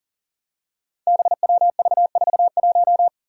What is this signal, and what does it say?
Morse code, heard as a keyed tone, about 40 wpm, 700 Hz. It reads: BWV41
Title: Five Challenging Letter-Number Combinations 40wpm